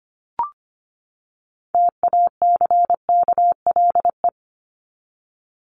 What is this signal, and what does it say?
Morse code, heard as a keyed tone, about 25 wpm, 700 Hz. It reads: TACKLE
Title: Top 1751-2000 Words 25wpm